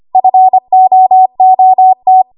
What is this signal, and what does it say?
Morse code, heard as a keyed tone, about 25 wpm, 750 Hz. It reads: FOOT
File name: Archive / 25 WPM / 25WPM